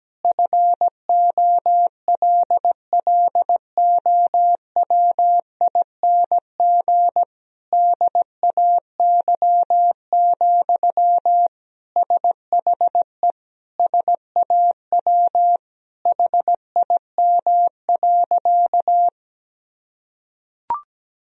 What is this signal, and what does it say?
Morse code, heard as a keyed tone, about 17 wpm, 700 Hz. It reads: FOLLOWING DAY, SHE SAW HIM.
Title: Sentences from Top 1000 Words - Review Part 4 - 17wpm